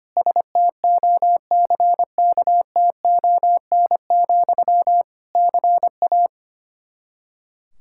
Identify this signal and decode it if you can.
Morse code, heard as a keyed tone, about 25 wpm, 700 Hz. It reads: STOCKTON, CA
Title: K1USN SST QSO Practice 25wpm